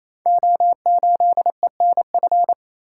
Morse code, heard as a keyed tone, about 29 wpm, 700 Hz. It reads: O8ENF
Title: Code Groups - 5 Letter:Numbers - 28wpm